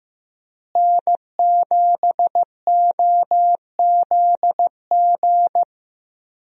Morse code, heard as a keyed tone, about 15 wpm, 700 Hz. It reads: N7OZG
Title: USA Call Signs - 5 Characters - Repeated 3x - 15wpm